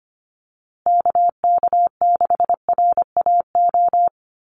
Morse code, heard as a keyed tone, about 25 wpm, 700 Hz. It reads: KK6RAO
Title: USA Call Signs Repeated 3x 25wpm